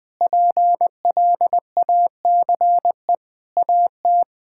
Morse code, heard as a keyed tone, about 20 wpm, 700 Hz. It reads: PLACE AT
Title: Sentences from Top 1600 Words 20wpm